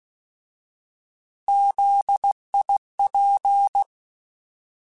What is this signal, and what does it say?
Morse code, heard as a keyed tone, about 16 wpm, 800 Hz. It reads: ZIP
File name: Alphabetical / Alphabetical mp3s / Zs @15